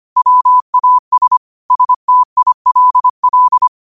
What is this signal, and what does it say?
Morse code, heard as a keyed tone, about 25 wpm, 1000 Hz. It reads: WAS STILL
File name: qotdfiles / 25wpm